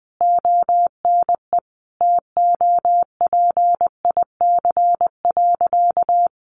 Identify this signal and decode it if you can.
Morse code, heard as a keyed tone, about 20 wpm, 700 Hz. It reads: ONE TOPIC.
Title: Sentences from Top 1200 Words 20wpm